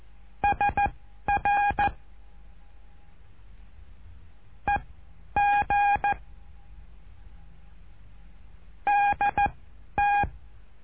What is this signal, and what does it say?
Morse code, heard as a keyed tone, about 14 wpm, spaced as for 10 wpm, 820 Hz. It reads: SR EG DT
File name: cursos / CURSO DE TELEGRAFIA / Telegrafia - Trilha 08